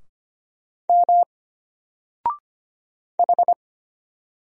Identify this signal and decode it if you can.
Morse code, heard as a keyed tone, about 25 wpm, 700 Hz. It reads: M H
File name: Single-Letter-Number-25wpm